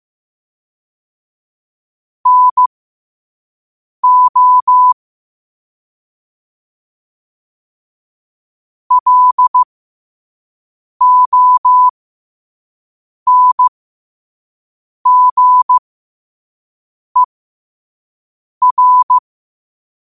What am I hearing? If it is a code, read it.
Morse code, heard as a keyed tone, about 15 wpm, spaced as for 5 wpm, 1000 Hz. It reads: NO LONGER
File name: qotdfiles / 5wpm